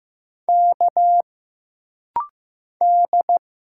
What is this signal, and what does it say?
Morse code, heard as a keyed tone, about 15 wpm, 700 Hz. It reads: K D
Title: Single Letters - Rapid-Fire 15wpm